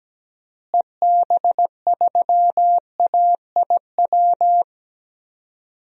Morse code, heard as a keyed tone, about 17 wpm, 700 Hz. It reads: EB3AIW
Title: DXCC-100 Call Signs - 6 Characters - Repeated 3x - 17wpm